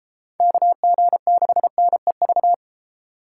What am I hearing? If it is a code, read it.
Morse code, heard as a keyed tone, about 36 wpm, 700 Hz. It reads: KG6NEV